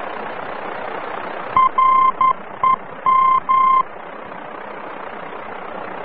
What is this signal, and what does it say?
Morse code, heard as a keyed tone, about 11 wpm, 1050 Hz, with noise. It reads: REM
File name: france / mx / REM-VOR-F